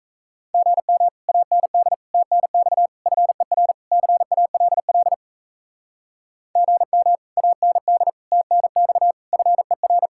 Morse code, heard as a keyed tone, about 38 wpm, 700 Hz. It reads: GM AND TNX FER CALL GM AND TNX FER